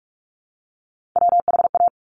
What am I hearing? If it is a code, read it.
Morse code, heard as a keyed tone, about 46 wpm, 690 Hz. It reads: WHA
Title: Top Three Letter Combinations in Words 45wpm